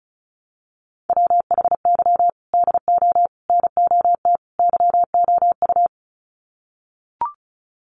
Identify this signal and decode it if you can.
Morse code, heard as a keyed tone, about 34 wpm, 700 Hz. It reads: WHY DO NOT YOU
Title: Sets of 4 Words 35wpm